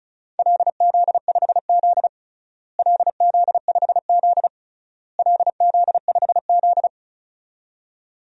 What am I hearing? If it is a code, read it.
Morse code, heard as a keyed tone, about 36 wpm, 700 Hz. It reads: LZ5Z LZ5Z LZ5Z
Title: DXCC-100 Call Signs - 4 Characters - Repeated 3x - 35wpm